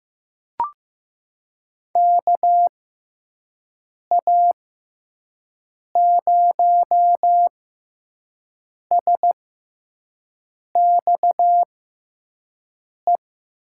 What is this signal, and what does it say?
Morse code, heard as a keyed tone, about 15 wpm, spaced as for 5 wpm, 700 Hz. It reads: KA0SXE
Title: USA Call Signs - 6x Character Spacing - 15wpm